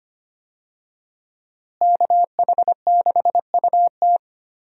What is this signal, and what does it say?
Morse code, heard as a keyed tone, about 25 wpm, 700 Hz. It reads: KH6UT
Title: USA Call Signs - 5 Characters - Repeated 2x - 25wpm